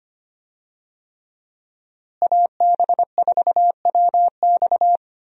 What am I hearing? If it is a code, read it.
Morse code, heard as a keyed tone, about 25 wpm, 700 Hz. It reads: AB4WX
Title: USA Call Signs - 5 Characters - Repeated 3x - 25wpm